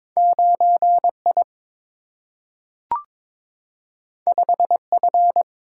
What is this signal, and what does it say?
Morse code, heard as a keyed tone, about 22 wpm, 700 Hz. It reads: 9 I 5 F
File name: practice / 22wpm / Two Letter-Number Combinations - 22wpm